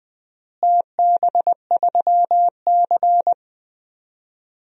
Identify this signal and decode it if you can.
Morse code, heard as a keyed tone, about 20 wpm, 700 Hz. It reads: TB3C
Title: DXCC-100 Call Signs - 4 Characters - Repeated 2x - 20wpm